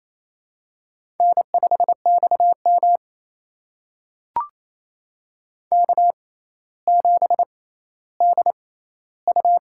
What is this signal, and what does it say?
Morse code, heard as a keyed tone, about 27 wpm, spaced as for 14 wpm, 700 Hz. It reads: N5XM K7DU
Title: USA Call Signs - 6x Character Spacing - 28wpm